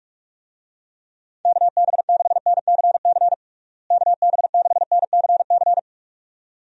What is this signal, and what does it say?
Morse code, heard as a keyed tone, about 46 wpm, 700 Hz. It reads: KB6NCC KB6NCC
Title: USA Call Signs - 6 Characters - Repeated 2x - 45wpm